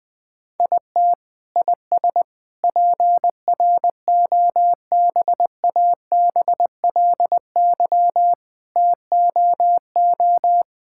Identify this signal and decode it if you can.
Morse code, heard as a keyed tone, about 20 wpm, 700 Hz. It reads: IT IS PROBABLY TOO
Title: Sentences from Top 1000 Words - Review Part 6 - 20wpm